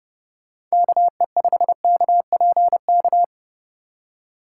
Morse code, heard as a keyed tone, about 30 wpm, 700 Hz. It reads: KE5KPK